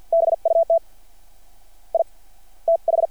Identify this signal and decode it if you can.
Morse code, heard as a keyed tone, about 52 wpm, spaced as for 24 wpm, 660 Hz. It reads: BUT IT H
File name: archived / CRSnet